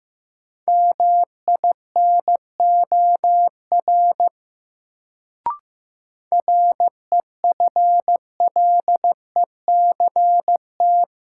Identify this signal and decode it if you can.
Morse code, heard as a keyed tone, about 15 wpm, 700 Hz. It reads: MINOR REFLECT
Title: Sentences from Top 1500 Words - New Word Drill Alt - 15wpm